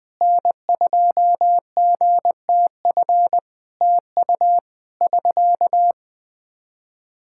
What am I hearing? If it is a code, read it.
Morse code, heard as a keyed tone, about 21 wpm, 700 Hz. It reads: N2GTF TU <SK>
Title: Practice QSO - Exchange 4 - Ending - 20wpm